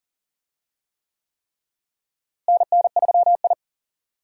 Morse code, heard as a keyed tone, about 39 wpm, 700 Hz. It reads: NN3I
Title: USA Call Signs - 4 Characters - Repeated 3x - 40wpm